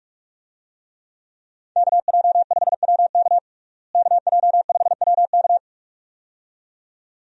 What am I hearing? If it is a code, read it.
Morse code, heard as a keyed tone, about 41 wpm, 700 Hz. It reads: KJ5WK KJ5WK
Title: USA Call Signs - 5 Characters - Repeated 2x - 45wpm